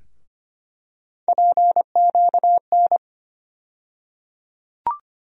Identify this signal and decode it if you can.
Morse code, heard as a keyed tone, about 25 wpm, 700 Hz. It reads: PQN
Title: USA Call Sign Postfixes - 3 Letters 25wpm